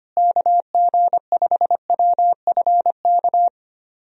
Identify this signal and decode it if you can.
Morse code, heard as a keyed tone, about 25 wpm, 700 Hz. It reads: KG5WFK
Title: USA Call Signs - 6 Characters - Repeated 2x - 25wpm